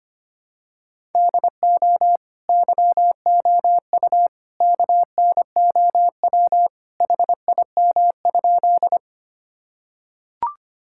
Morse code, heard as a keyed tone, about 25 wpm, 700 Hz. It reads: DO YOU KNOW HIM?
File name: practice / 25wpm / Sentences with 2-Word Phrases - 25wpm